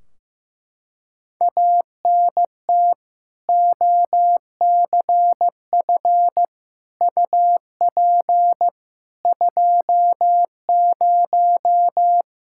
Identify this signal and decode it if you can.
Morse code, heard as a keyed tone, about 15 wpm, 700 Hz. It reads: ANT OCF UP 20
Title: K1USN SST QSO Practice 15wpm